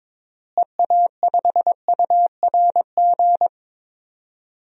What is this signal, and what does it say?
Morse code, heard as a keyed tone, about 22 wpm, 700 Hz. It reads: EA5URG
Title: DXCC-100 Call Signs - 6 Characters - Repeated 3x - 22wpm